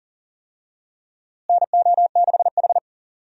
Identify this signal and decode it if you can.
Morse code, heard as a keyed tone, about 40 wpm, 700 Hz. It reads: NO6H
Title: USA Call Signs - 4 Characters - Repeated 2x - 40wpm